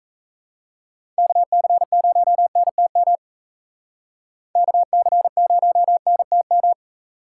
Morse code, heard as a keyed tone, about 36 wpm, 700 Hz. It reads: KC0NTM KC0NTM